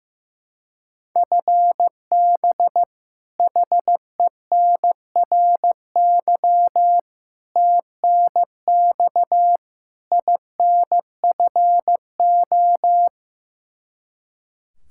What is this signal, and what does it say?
Morse code, heard as a keyed tone, about 15 wpm, 700 Hz. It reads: FB HENRY TNX INFO
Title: Practice QSO - Exchange 3 - More Info - 15wpm